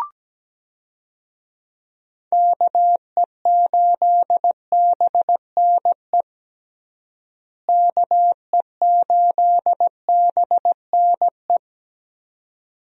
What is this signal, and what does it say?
Morse code, heard as a keyed tone, about 17 wpm, 700 Hz. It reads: KE8BNE KE8BNE
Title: USA Call Signs - 6 Characters - Repeated 2x - 17wpm